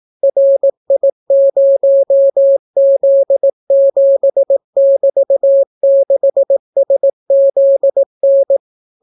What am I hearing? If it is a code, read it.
Morse code, heard as a keyed tone, about 18 wpm, 550 Hz. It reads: RI0Z7<BT>6SZN